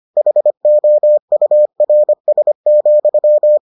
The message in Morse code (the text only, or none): HOURS,